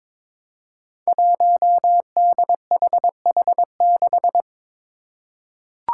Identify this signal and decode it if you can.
Morse code, heard as a keyed tone, about 22 wpm, 700 Hz. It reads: 1DHH6
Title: Five Challenging Letter-Number Combinations 22wpm